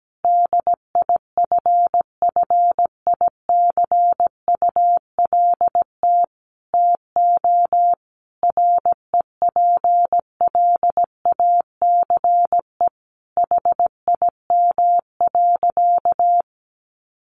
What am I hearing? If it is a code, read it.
Morse code, heard as a keyed tone, about 17 wpm, 700 Hz. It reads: DIFFICULT TO REPLACE HIM.
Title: Sentences from Top 1400 Words 17wpm